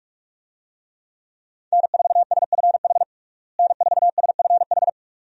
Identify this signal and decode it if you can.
Morse code, heard as a keyed tone, about 46 wpm, 700 Hz. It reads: N4SFH N4SFH
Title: USA Call Signs - 5 Characters - Repeated 2x - 45wpm